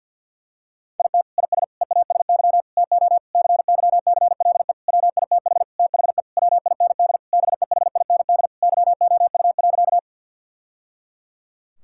Morse code, heard as a keyed tone, about 52 wpm, 700 Hz. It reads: IT IS EASY TO CYCLE WITH THE WIND BEHIND YOU.
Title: Sentences from Top 1000 Words - Review Part 6 - 50wpm